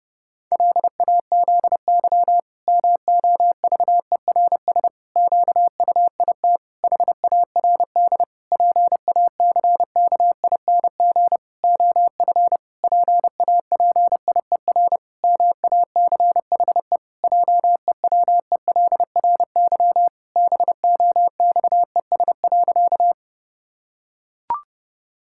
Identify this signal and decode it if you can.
Morse code, heard as a keyed tone, about 30 wpm, 700 Hz. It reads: LAZY MOVERS QUIT HARD PACKING OF PAPIER MACHE JEWELRY BOXES.
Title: Pangram Challenge v2 - 30wpm